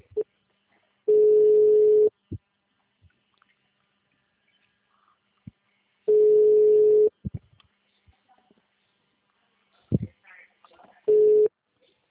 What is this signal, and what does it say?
Ringback tone (Europe, 425 Hz), with speech.
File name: out- -2012- - - .44559